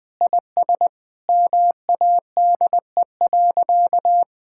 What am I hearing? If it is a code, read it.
Morse code, heard as a keyed tone, about 20 wpm, 700 Hz. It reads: IS MADE.
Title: Sentences with 5-Word Phrases - 20wpm